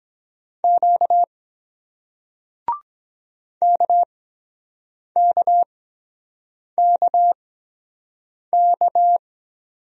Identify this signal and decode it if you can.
Morse code, heard as a keyed tone, about 23 wpm, 700 Hz. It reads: Q K K K K